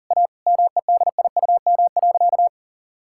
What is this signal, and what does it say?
Morse code, heard as a keyed tone, about 40 wpm, 700 Hz. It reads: A MEDIUM.
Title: Sentences from Top 1000 Words - Review Part 6 - 40wpm